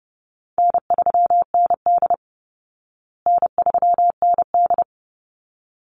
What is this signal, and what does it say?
Morse code, heard as a keyed tone, about 30 wpm, 700 Hz. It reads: N3ND N3ND
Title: Easy Call Signs Repeated 2x - 30wpm